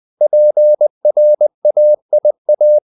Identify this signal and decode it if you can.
Morse code, heard as a keyed tone, about 20 wpm, 600 Hz. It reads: PRAIA
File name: CW / Practice / NationalCapitals20